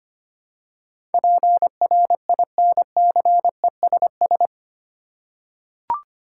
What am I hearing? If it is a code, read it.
Morse code, heard as a keyed tone, about 25 wpm, 700 Hz. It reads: PRINCESS
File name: 8 Letter Words - First 3 Letters Given - 25wpm